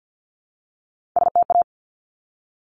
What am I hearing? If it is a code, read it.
Morse code, heard as a keyed tone, about 50 wpm, 700 Hz. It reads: STA
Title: Top Three Letter Combinations in Words 50wpm